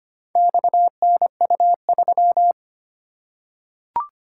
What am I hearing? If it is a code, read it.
Morse code, heard as a keyed tone, about 25 wpm, 700 Hz. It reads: XNU3
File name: practice / 25wpm / Four Letter-Number Combinations - 25wpm